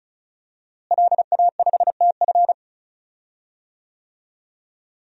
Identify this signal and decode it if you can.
Morse code, heard as a keyed tone, about 36 wpm, 700 Hz. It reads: LA5TF
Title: DXCC-100 Call Signs - 5 Characters - Repeated 2x - 35wpm